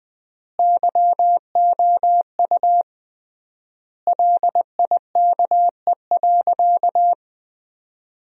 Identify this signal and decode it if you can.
Morse code, heard as a keyed tone, about 20 wpm, 700 Hz. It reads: YOU LIKE.
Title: 3x Word-Spacing - Sentences from Top 500 Words - 20wpm